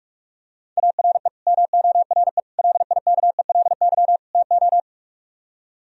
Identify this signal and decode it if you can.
Morse code, heard as a keyed tone, about 45 wpm, 700 Hz. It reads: ARE MORE LIKELY TO